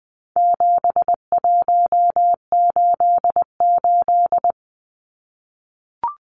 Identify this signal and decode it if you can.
Morse code, heard as a keyed tone, about 20 wpm, 700 Hz. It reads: 7188
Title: Numbers - 4-Digit 20wpm